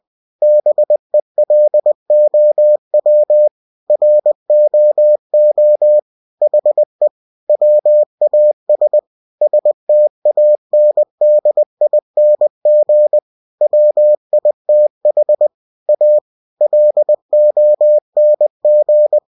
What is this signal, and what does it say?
Morse code, heard as a keyed tone, about 20 wpm, 600 Hz. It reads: BELOW ROO HE WAS STANDING WITH A LONG
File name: winnie-20wpm